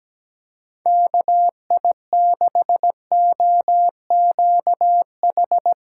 Morse code, heard as a keyed tone, about 17 wpm, 700 Hz. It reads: KI6OQH
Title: USA Call Signs - 6 Characters - 6x Character Spacing - 17wpm